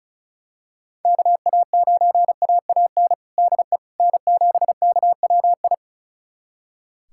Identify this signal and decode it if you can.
Morse code, heard as a keyed tone, about 34 wpm, 700 Hz. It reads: KA9AAN DE N7KWI
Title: USA Callsign DE Callsign 35wpm